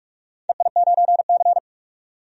Morse code, heard as a keyed tone, about 46 wpm, 700 Hz. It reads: EI9C
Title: DXCC-100 Call Signs - 4 Characters - 45wpm